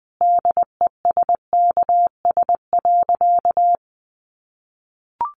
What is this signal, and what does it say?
Morse code, heard as a keyed tone, about 19 wpm, 700 Hz. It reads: DESKS.
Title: Additional Easy Phrases 20wpm